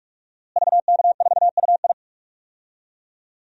Morse code, heard as a keyed tone, about 41 wpm, 700 Hz. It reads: VK4UI
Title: DXCC-100 Call Signs - 5 Characters - Repeated 2x - 45wpm